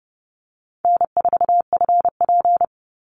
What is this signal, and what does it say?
Morse code, heard as a keyed tone, about 30 wpm, 700 Hz. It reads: N4FP
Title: Call Signs Repeated 2x - 30wpm